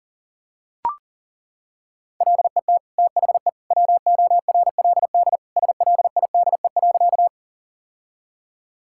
Morse code, heard as a keyed tone, about 37 wpm, 700 Hz. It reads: LET THE WORLD SLIDE.
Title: Sentences from Top 1600 Words 40wpm